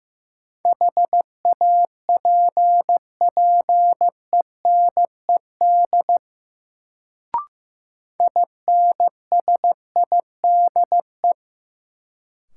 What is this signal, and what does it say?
Morse code, heard as a keyed tone, about 15 wpm, 700 Hz. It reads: HAPPENED INSIDE
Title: Sentences from Top 1000 Words - Review Part 2 - New Word Drill - 15wpm